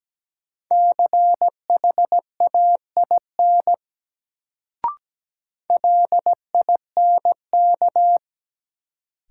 Morse code, heard as a keyed tone, about 17 wpm, 700 Hz. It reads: CHAIN LINK
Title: Sentences from Top 1500 Words - New Word Drill Alt - 17wpm